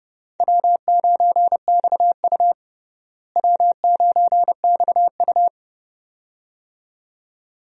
Morse code, heard as a keyed tone, about 32 wpm, 700 Hz. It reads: W9XU W9XU
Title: USA Call Signs - 4 Characters - Repeated 2x - 30wpm